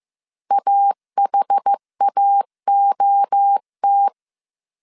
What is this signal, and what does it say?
Morse code, heard as a keyed tone, about 15 wpm, 800 Hz. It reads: AHAOT
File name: CW / DET / 1 3 1500C S A 14wpm 2447